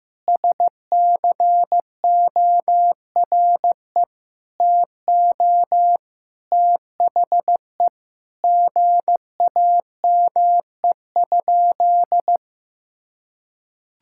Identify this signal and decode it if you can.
Morse code, heard as a keyed tone, about 15 wpm, 700 Hz. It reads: SCORE TO THE GAME?
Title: Sentences from Top 1600 Words 15wpm